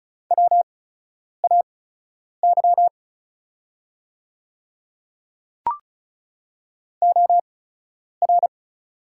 Morse code, heard as a keyed tone, about 36 wpm, spaced as for 10 wpm, 700 Hz. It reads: WAY OR
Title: 8x Character Spacing - Top 200 Words - 35wpm